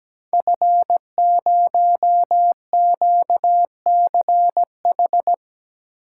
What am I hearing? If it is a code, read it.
Morse code, heard as a keyed tone, about 17 wpm, 700 Hz. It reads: F0QCH
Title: Code Groups - 5 Letter:Numbers - 17wpm